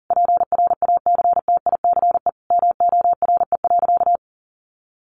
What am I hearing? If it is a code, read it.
Morse code, heard as a keyed tone, about 40 wpm, 700 Hz. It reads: PRACTICE MORE.
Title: Sentences from Top 700 Words 40wpm